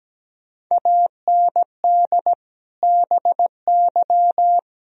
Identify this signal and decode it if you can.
Morse code, heard as a keyed tone, about 17 wpm, 700 Hz. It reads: AND BY
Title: Sentences from Top 1000 Words - Review Part 4 - 17wpm